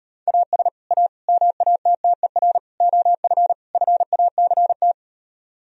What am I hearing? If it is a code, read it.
Morse code, heard as a keyed tone, about 39 wpm, 700 Hz. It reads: AS A MATTER OF FACT